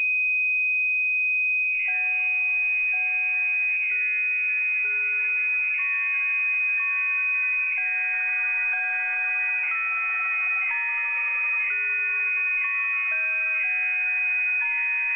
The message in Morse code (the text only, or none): none